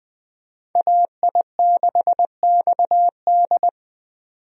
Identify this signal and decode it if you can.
Morse code, heard as a keyed tone, about 20 wpm, 700 Hz. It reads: AI6XD
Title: USA Call Signs - 5 Characters - 20wpm